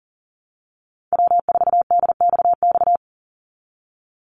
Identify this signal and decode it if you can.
Morse code, heard as a keyed tone, about 40 wpm, 700 Hz. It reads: W4DXX
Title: Call Signs Repeated 2x - 40wpm